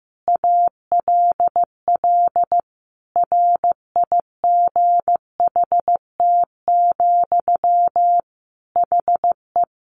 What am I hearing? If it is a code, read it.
Morse code, heard as a keyed tone, about 15 wpm, 700 Hz. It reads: ALL RIGHT, HE
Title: Sentences from Top 1100 Words 15wpm